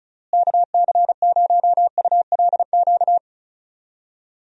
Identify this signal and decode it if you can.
Morse code, heard as a keyed tone, about 34 wpm, 700 Hz. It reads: KC0ULQ
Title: USA Call Signs - 6 Characters - Repeated 3x - 35wpm